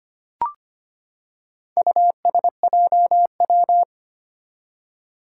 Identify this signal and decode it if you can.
Morse code, heard as a keyed tone, about 25 wpm, 700 Hz. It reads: USJW
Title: Four Challenging Letter-Number Combinations 25wpm